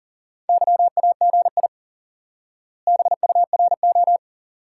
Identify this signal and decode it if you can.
Morse code, heard as a keyed tone, about 40 wpm, 700 Hz. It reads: YAGI BURO
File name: IQE Course 3 - Lesson 89 - YAGI - Two Elements - 40wpm